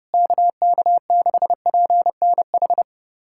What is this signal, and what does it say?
Morse code, heard as a keyed tone, about 30 wpm, 700 Hz. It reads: KK6PNH
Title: USA Callsign DE Callsign - 6 Characters - 30wpm